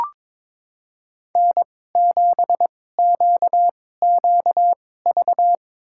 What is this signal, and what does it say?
Morse code, heard as a keyed tone, about 22 wpm, spaced as for 16 wpm, 700 Hz. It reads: N7QQV
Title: USA Call Signs - 5 Characters - 2x Character Spacing - 22wpm